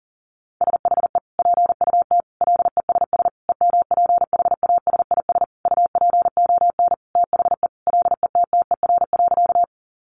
Morse code, heard as a keyed tone, about 40 wpm, 700 Hz. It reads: SHE PUT LESS EMPHASIS UPON THE LETTER.
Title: Sentences from Top 1400 Words 40wpm